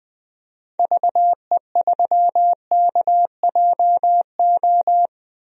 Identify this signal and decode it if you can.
Morse code, heard as a keyed tone, about 20 wpm, 700 Hz. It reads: VE3KJO
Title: DXCC-100 Call Signs - 6 Characters - Repeated 2x - 20wpm